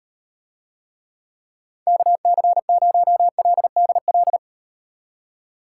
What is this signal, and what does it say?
Morse code, heard as a keyed tone, about 38 wpm, 700 Hz. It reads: KC0LDL